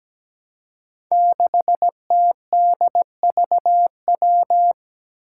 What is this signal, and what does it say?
Morse code, heard as a keyed tone, about 17 wpm, 700 Hz. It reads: 6TDVW
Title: Five Challenging Letter-Number Combinations 17wpm